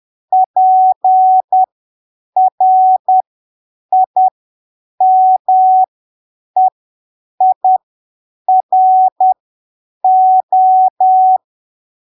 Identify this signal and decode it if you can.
Morse code, heard as a keyed tone, about 10 wpm, spaced as for 7 wpm, 750 Hz. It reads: PRIMEIRO
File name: arquivos / texto / texto 29 10ppm